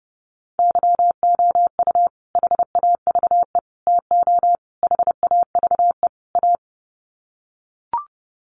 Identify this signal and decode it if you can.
Morse code, heard as a keyed tone, about 30 wpm, 700 Hz. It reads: YOU HAVE TO HAVE A
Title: Sets of 5 Words Encore 30wpm